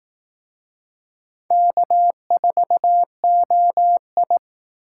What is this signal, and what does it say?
Morse code, heard as a keyed tone, about 18 wpm, 700 Hz. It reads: K4OI
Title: Speed Racing - USA 1x2 Call Signs 15wpm